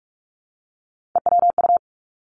Morse code, heard as a keyed tone, about 46 wpm, 700 Hz. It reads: EWU
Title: Three Challenging Letter-Number Combinations 45wpm